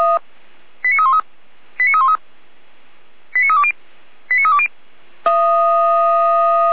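Five-tone selective call (ZVEI), heard as quick runs of five tones, tones of about 70 ms. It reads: 89313 89313 89320 89320
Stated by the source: ZVEI five-tone selective call received on an SDR.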